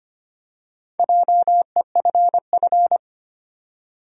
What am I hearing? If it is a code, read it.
Morse code, heard as a keyed tone, about 25 wpm, 700 Hz. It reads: JEFF
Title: USA Common Names - Max 4 Letters - 25wpm